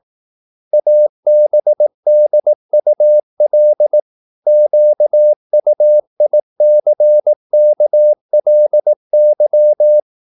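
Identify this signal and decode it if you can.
Morse code, heard as a keyed tone, about 18 wpm, 600 Hz. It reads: ABDUL QUICKLY
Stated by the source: Morse practice at 18 wpm, English text, from CW med Gnister.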